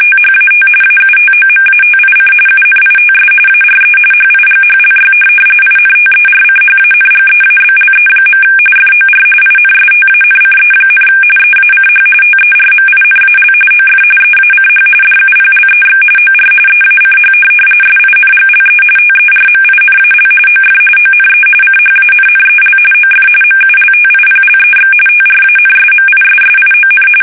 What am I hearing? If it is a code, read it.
RTTY, about 45.45 baud. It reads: THAT AUTOMATIC UNITS ARE HIGHLY INTERCHANGEABLE. FOR INSTANCE, EACH TERMINAL OF EACH DUPLEX OR EACH SINGLE TERMINAL OF ONE CHANNEL OF MULTIPLEX TRANSMISSION
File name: fsk850